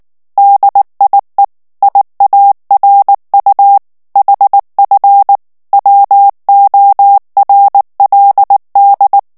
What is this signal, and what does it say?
Morse code, heard as a keyed tone, about 19 wpm, 800 Hz. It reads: DIE IARU HF WORLD
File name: morse mp3 / ALLEMAND texte clair 2 vitesse 18 mots minute